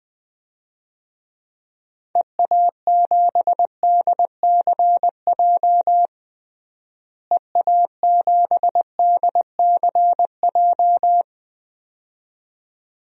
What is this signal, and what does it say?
Morse code, heard as a keyed tone, about 20 wpm, 700 Hz. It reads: EA7DCJ EA7DCJ
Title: DXCC-100 Call Signs - 6 Characters - Repeated 2x - 20wpm